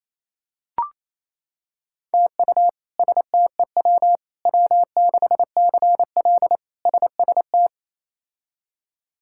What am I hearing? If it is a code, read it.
Morse code, heard as a keyed tone, about 28 wpm, 700 Hz. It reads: TU STEW W6CL SST
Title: K1USN SST QSO Practice - 28wpm